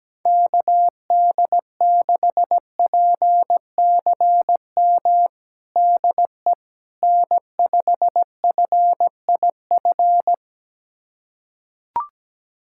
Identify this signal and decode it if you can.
Morse code, heard as a keyed tone, about 17 wpm, 700 Hz. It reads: KD6PCM DE N5FIF
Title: USA Callsign DE Callsign 17wpm